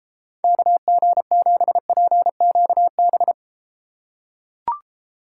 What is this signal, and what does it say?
Morse code, heard as a keyed tone, about 34 wpm, 700 Hz. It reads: KG7PQB